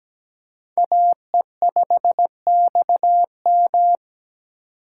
Morse code, heard as a keyed tone, about 17 wpm, 700 Hz. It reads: AE5XM
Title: USA Call Signs - 5 Characters - 2x Character Spacing - 17wpm